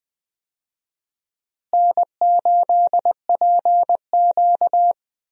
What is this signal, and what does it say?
Morse code, heard as a keyed tone, about 20 wpm, 700 Hz. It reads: N8PQ
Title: USA Call Signs - 4 Characters - Repeated 2x - 20wpm